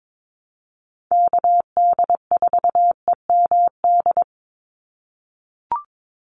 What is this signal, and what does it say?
Morse code, heard as a keyed tone, about 22 wpm, 700 Hz. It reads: KD4EMD
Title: USA Call Signs 22wpm